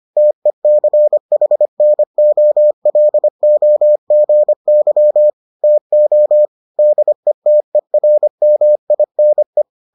Morse code, heard as a keyed tone, about 25 wpm, 600 Hz. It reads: TECHNOLOGY TO DETERMINE